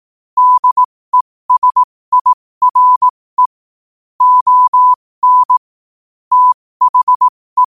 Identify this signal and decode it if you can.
Morse code, heard as a keyed tone, about 18 wpm, spaced as for 15 wpm, 1000 Hz. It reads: DESIRE ON THE
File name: qotdfiles / 15wpm